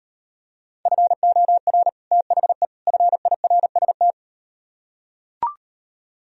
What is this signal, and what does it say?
Morse code, heard as a keyed tone, about 38 wpm, 700 Hz. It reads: FOR THE FIRST